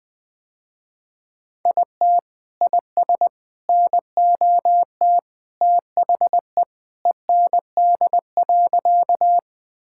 Morse code, heard as a keyed tone, about 20 wpm, 700 Hz. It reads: IT IS NOT THE END.
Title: Sentences with 3-Word Phrases - Spoken Beginning - 20wpm